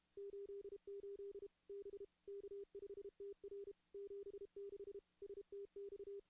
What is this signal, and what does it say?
Morse code, heard as a keyed tone, about 33 wpm, 400 Hz. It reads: 88 B K5TR 76 STX
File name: contests / CW-SS-k5tr- -2100